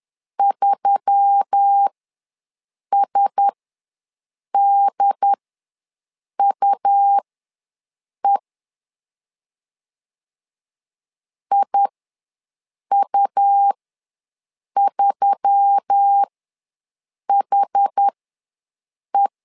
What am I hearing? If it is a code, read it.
Morse code, heard as a keyed tone, about 10 wpm, spaced as for 6 wpm, 800 Hz. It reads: 3SDUE IU3HE
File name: LEZIONI / 4 Q1S3 ws 10wpm 2437